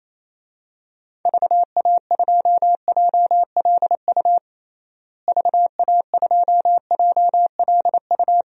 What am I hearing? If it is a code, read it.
Morse code, heard as a keyed tone, about 28 wpm, 700 Hz. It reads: VA2JLU VA2JLU
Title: DXCC-100 Call Signs - 6 Characters - Repeated 3x - 28wpm